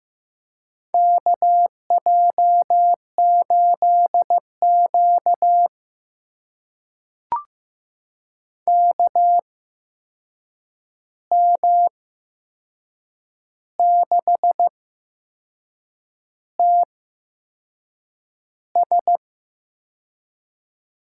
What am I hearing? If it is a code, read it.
Morse code, heard as a keyed tone, about 15 wpm, spaced as for 6 wpm, 700 Hz. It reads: KJ8Q KM6TS
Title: USA Call Signs - 8x Character Spacing - 15wpm